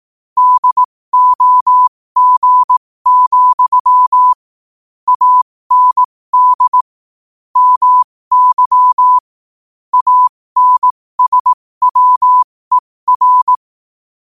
Morse code, heard as a keyed tone, about 18 wpm, spaced as for 15 wpm, 1000 Hz. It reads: DOG, AND MY ANSWER
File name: qotdfiles / 15wpm